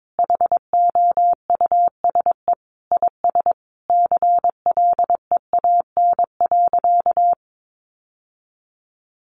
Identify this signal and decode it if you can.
Morse code, heard as a keyed tone, about 22 wpm, 700 Hz. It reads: HOUSE IS CLEAN.
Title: Sentences from Top 1100 Words 22wpm